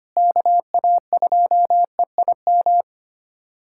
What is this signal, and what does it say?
Morse code, heard as a keyed tone, about 25 wpm, 700 Hz. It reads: KA2EIM
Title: USA Callsign DE Callsign - 6 Characters - 25wpm